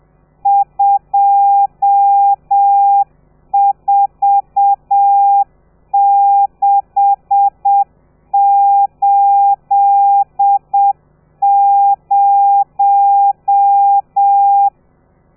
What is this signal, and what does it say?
Morse code, heard as a keyed tone, about 7 wpm, 800 Hz. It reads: 24680
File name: telegrafia / i2tfj / Lez13 30